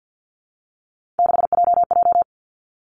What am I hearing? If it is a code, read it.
Morse code, heard as a keyed tone, about 50 wpm, 700 Hz. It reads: 6JJ
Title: Three Challenging Letter-Number Combinations 50wpm